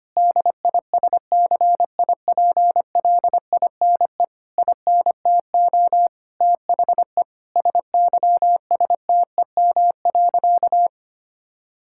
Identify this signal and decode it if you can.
Morse code, heard as a keyed tone, about 25 wpm, 700 Hz. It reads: DISCIPLINE INTO THE SYSTEM.
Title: Sentences from Top 1000 Words - Review Part 6 - 25wpm